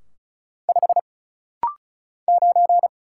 Morse code, heard as a keyed tone, about 34 wpm, 700 Hz. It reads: 5 9
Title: Single Letter-Number - Rapid-Fire 35wpm